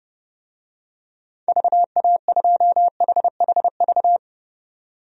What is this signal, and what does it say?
Morse code, heard as a keyed tone, about 30 wpm, 700 Hz. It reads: VA2HHV
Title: DXCC-100 Call Signs - 6 Characters - 30wpm